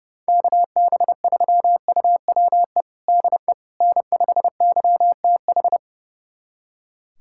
Morse code, heard as a keyed tone, about 30 wpm, 700 Hz. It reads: KB3UWE DE N5YTH
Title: USA Callsign DE Callsign 30wpm